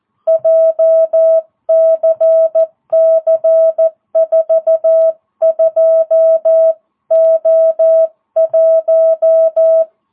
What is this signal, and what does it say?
Morse code, heard as a keyed tone, about 14 wpm, 660 Hz. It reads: JCC42O1